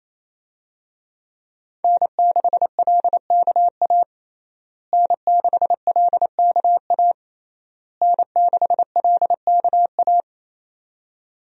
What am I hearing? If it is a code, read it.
Morse code, heard as a keyed tone, about 28 wpm, 700 Hz. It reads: N6LKA N6LKA N6LKA
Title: USA Call Signs - 5 Characters - Repeated 3x - 28wpm